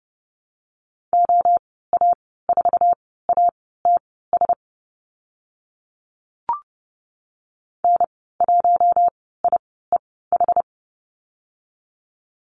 Morse code, heard as a keyed tone, about 30 wpm, spaced as for 17 wpm, 700 Hz. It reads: OA4ATS N1IEH
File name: Lesson 14g - Number 2 - Call Signs 30wpm